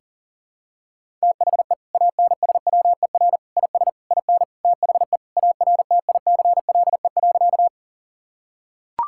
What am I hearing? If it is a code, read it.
Morse code, heard as a keyed tone, about 37 wpm, 700 Hz. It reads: THE ANSWER IS IN THE ARTICLE.
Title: Sentences from Top 1000 Words - Review Part 5 - 40wpm